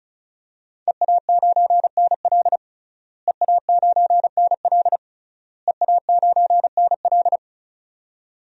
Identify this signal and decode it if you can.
Morse code, heard as a keyed tone, about 36 wpm, 700 Hz. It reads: EA9NL EA9NL EA9NL
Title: DXCC-100 Call Signs - 5 Characters - Repeated 3x - 35wpm